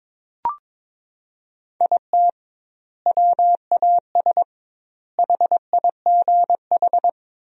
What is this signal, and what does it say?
Morse code, heard as a keyed tone, about 21 wpm, 700 Hz. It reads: IT WAS HIGH
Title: 2x Word-Spacing - Sentences from Top 200 Words - 22wpm